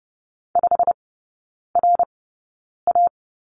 Morse code, heard as a keyed tone, about 30 wpm, spaced as for 9 wpm, 700 Hz. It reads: 5RA
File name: course / 30:10wpm / 087 Lesson 15e - Number 5 - Three Characters 30wpm